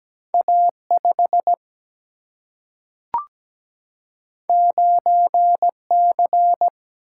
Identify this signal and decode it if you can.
Morse code, heard as a keyed tone, about 17 wpm, 700 Hz. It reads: A 5 9 C
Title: Single-Letter Single-Number 17wpm